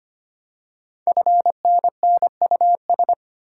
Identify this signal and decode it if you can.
Morse code, heard as a keyed tone, about 25 wpm, 700 Hz. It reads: FNNUS
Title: Code Groups - 5 Letters - 25wpm